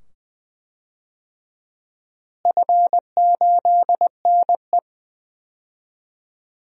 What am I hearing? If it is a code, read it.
Morse code, heard as a keyed tone, about 21 wpm, 700 Hz. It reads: F8NE
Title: DXCC-100 Call Signs - 4 Characters - Repeated 2x - 20wpm